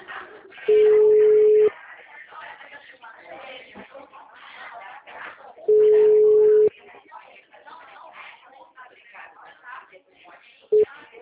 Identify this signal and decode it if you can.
Ringback tone (Europe, 425 Hz), with speech.